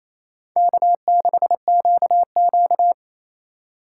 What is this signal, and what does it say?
Morse code, heard as a keyed tone, about 34 wpm, 700 Hz. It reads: K6QQ
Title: Speed Racing - USA 1x2 Call Signs 28wpm